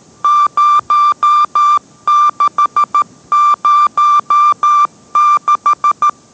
Morse code, heard as a keyed tone, about 13 wpm, 1180 Hz. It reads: 0606